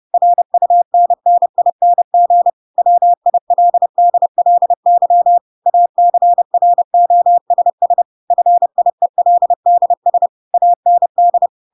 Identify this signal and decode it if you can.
Morse code, heard as a keyed tone, about 30 wpm, 700 Hz. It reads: RUNNING WILDLY ACROSS FIELDS AND